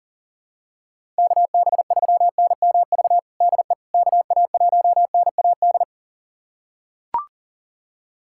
Morse code, heard as a keyed tone, about 40 wpm, 700 Hz. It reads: KB3NMV DE KA1NAD
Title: USA Callsign DE Callsign - 6 Characters - 40wpm